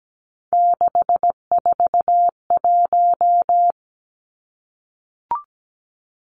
Morse code, heard as a keyed tone, about 17 wpm, 700 Hz. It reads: 641
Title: Three Challenging Letter-Number Combinations 17wpm